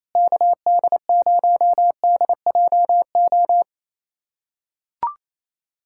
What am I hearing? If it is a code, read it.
Morse code, heard as a keyed tone, about 28 wpm, 700 Hz. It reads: KD0DJO
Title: USA Callsign DE Callsign 28wpm